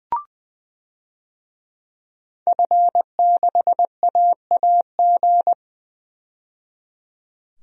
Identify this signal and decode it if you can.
Morse code, heard as a keyed tone, about 20 wpm, 700 Hz. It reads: F6AAG
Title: DXCC-100 Call Signs - 5 Characters - 20wpm